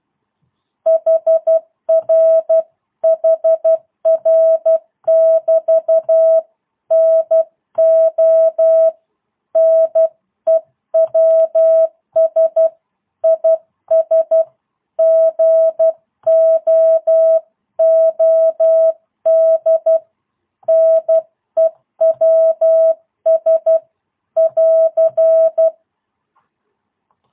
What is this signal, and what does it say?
Morse code, heard as a keyed tone, about 12 wpm, 650 Hz. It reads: HR HR <BT> NO NEWS IS GOOD NEWS <AR>
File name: jh8cbh / 2022cw